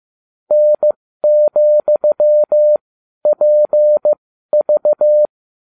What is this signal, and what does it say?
Morse code, heard as a keyed tone, about 15 wpm, spaced as for 12 wpm, 600 Hz. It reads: N,PV